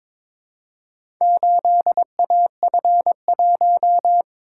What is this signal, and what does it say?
Morse code, heard as a keyed tone, about 22 wpm, 700 Hz. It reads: 8AF1
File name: practice / 22wpm / Four Letter-Number Combinations - 22wpm